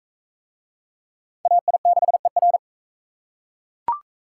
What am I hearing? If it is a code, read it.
Morse code, heard as a keyed tone, about 38 wpm, 700 Hz. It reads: AI6ER